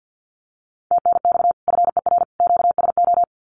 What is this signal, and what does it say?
Morse code, heard as a keyed tone, about 50 wpm, 700 Hz. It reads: TNX FER QSO
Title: Common QSO Phrases 50wpm